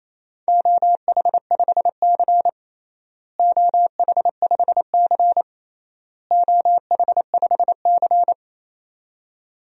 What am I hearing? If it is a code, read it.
Morse code, heard as a keyed tone, about 28 wpm, 700 Hz. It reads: OH5C OH5C OH5C
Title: DXCC-100 Call Signs - 4 Characters - Repeated 3x - 28wpm